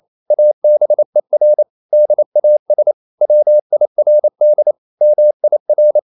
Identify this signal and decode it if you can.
Morse code, heard as a keyed tone, about 28 wpm, 600 Hz. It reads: ABER DAS WIRD MIR
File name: Liebespaar4 28wpm